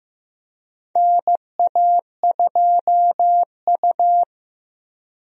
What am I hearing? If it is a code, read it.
Morse code, heard as a keyed tone, about 15 wpm, 700 Hz. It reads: NA2U
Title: USA Call Signs - 4 Characters - Repeated 2x - 15wpm